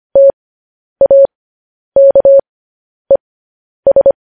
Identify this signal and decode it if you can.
Morse code, heard as a keyed tone, about 25 wpm, spaced as for 10 wpm, 550 Hz. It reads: TAKES
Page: news headlines in morse code at 10 wpm